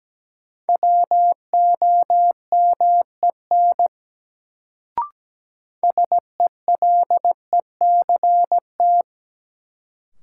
Morse code, heard as a keyed tone, about 17 wpm, 700 Hz. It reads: WOMEN SELECT
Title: Sentences from Top 1300 Words - New Word Drill - 17wpm